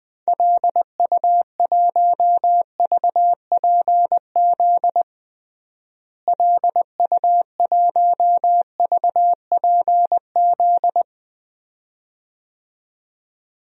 Morse code, heard as a keyed tone, about 20 wpm, 700 Hz. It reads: LU1VPZ LU1VPZ
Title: DXCC-100 Call Signs - 6 Characters - Repeated 2x - 20wpm